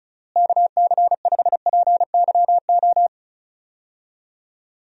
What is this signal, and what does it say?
Morse code, heard as a keyed tone, about 34 wpm, 700 Hz. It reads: KC5PYO
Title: USA Call Signs - 6 Characters - Repeated 3x - 35wpm